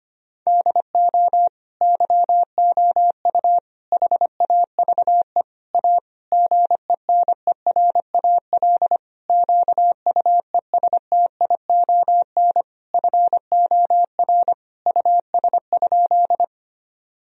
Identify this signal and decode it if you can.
Morse code, heard as a keyed tone, about 25 wpm, 700 Hz. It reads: DO YOU HAVE A GENERAL QUESTION FOR US?
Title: Sentences from Top 1000 Words - Review Part 3 - 25wpm